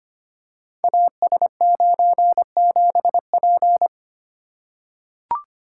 Morse code, heard as a keyed tone, about 25 wpm, 700 Hz. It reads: AS97P
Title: Code Groups - 5 Letter:Numbers - 25wpm